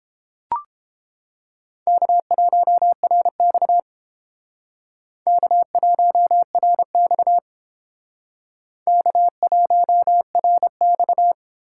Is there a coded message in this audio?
Yes: Morse code from 1.9 to 11.3 s, heard as a keyed tone, about 31 wpm, 700 Hz. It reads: K1RX K1RX K1RX